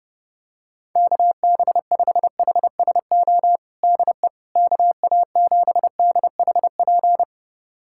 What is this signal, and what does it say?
Morse code, heard as a keyed tone, about 30 wpm, 700 Hz. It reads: KB5HSO DE KA7DHP
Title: USA Callsign DE Callsign 30wpm